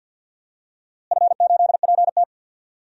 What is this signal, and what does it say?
Morse code, heard as a keyed tone, about 51 wpm, 700 Hz. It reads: F8PT
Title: DXCC-100 Call Signs - 4 Characters - Repeated 3x - 50wpm